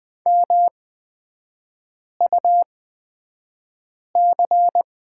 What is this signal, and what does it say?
Morse code, heard as a keyed tone, about 20 wpm, spaced as for 5 wpm, 700 Hz. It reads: MUC
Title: 224 Lesson 36e - BT - Three Characters 20wpm